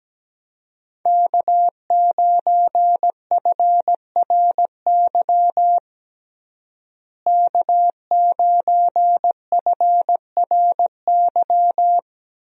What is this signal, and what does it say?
Morse code, heard as a keyed tone, about 17 wpm, 700 Hz. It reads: K9FRY K9FRY
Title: USA Call Signs - 5 Characters - Repeated 3x - 17wpm